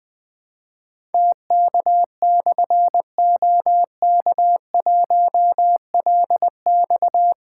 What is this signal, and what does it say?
Morse code, heard as a keyed tone, about 20 wpm, 700 Hz. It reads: TK/OK1LX
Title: Difficult Call Signs 20wpm